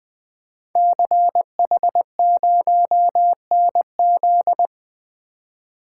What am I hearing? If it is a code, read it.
Morse code, heard as a keyed tone, about 20 wpm, 700 Hz. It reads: CH0NZ
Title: Code Groups - 5 Letter:Numbers - 20wpm